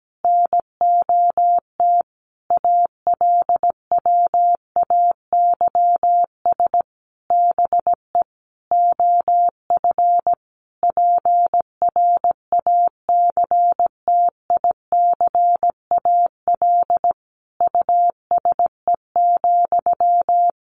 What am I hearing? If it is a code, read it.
Morse code, heard as a keyed tone, about 17 wpm, 700 Hz. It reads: NOT ALWAYS BE OF PRACTICAL USE,
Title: Sentences from Top 1400 Words 17wpm